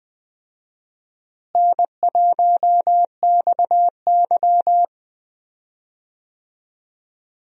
Morse code, heard as a keyed tone, about 20 wpm, 700 Hz. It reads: N1XY
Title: USA Call Signs - 4 Characters - Repeated 3x - 20wpm